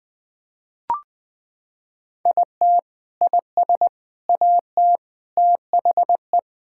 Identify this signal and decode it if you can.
Morse code, heard as a keyed tone, about 21 wpm, 700 Hz. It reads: IT IS AT THE
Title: Sentences with 4-Word Phrases - 20wpm